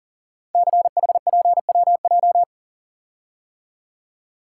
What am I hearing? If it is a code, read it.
Morse code, heard as a keyed tone, about 40 wpm, 700 Hz. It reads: CHPWJ
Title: Code Groups - 5 Letters - 40wpm